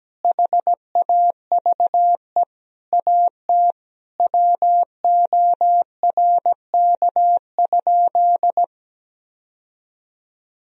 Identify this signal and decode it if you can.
Morse code, heard as a keyed tone, about 17 wpm, 700 Hz. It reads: HAVE AT WORK?
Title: Sentences from Top 1000 Words - Review Part 4 - 17wpm